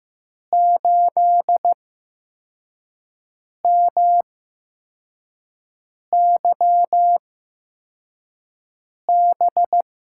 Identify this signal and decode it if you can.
Morse code, heard as a keyed tone, about 15 wpm, spaced as for 4 wpm, 700 Hz. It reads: 8MYB
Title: USA Call Signs - 8x Character Spacing - 15wpm